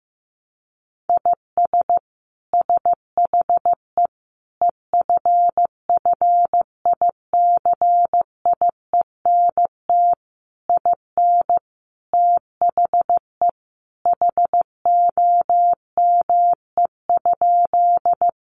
Morse code, heard as a keyed tone, about 14 wpm, 700 Hz. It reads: IS SHE EFFICIENT IN THE HOME?
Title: Sentences from Top 1300 Words 15wpm